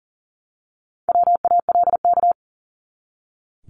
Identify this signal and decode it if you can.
Morse code, heard as a keyed tone, about 40 wpm, 700 Hz. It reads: WALK
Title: ING Suffix - Top 100 Words with Base Words 40wpm